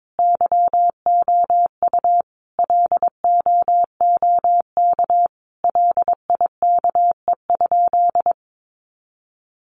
Morse code, heard as a keyed tone, about 22 wpm, 700 Hz. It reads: YOU LOOK LIKE?
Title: Sentences from Top 100 Words 22wpm